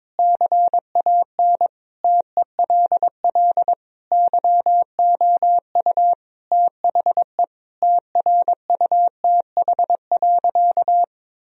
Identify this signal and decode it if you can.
Morse code, heard as a keyed tone, about 22 wpm, 700 Hz. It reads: CAN TELL YOU THE TRUTH.
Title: Sentences from Top 1000 Words - Review Part 2 - 22wpm